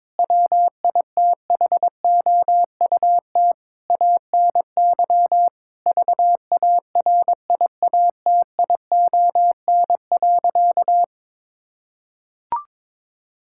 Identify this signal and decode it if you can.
Morse code, heard as a keyed tone, about 22 wpm, 700 Hz. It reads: WITHOUT ANY VARIATION.
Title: Sentences from Top 1600 Words 22wpm